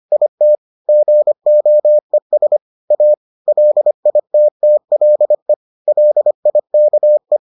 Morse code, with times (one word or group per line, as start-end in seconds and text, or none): 0.1-0.6 IT
0.9-2.6 GOES
2.9-3.1 A
3.5-5.5 LITTLE
5.9-7.4 LIKE